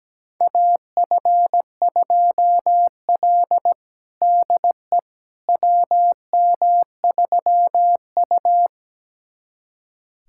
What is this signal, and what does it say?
Morse code, heard as a keyed tone, about 17 wpm, 700 Hz. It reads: AF2L DE WM3U
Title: USA Callsign DE Callsign - 4 Characters - 17wpm